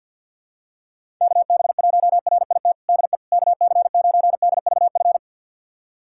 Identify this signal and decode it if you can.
Morse code, heard as a keyed tone, about 52 wpm, 700 Hz. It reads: KB1RIT DE KC9DVF
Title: USA Callsign DE Callsign - 6 Characters - 50wpm